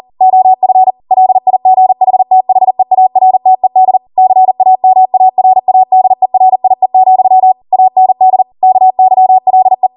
Morse code, heard as a keyed tone, about 40 wpm, 750 Hz. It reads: OF LIGHTHEARTED CAMARADERIE, AND KYLE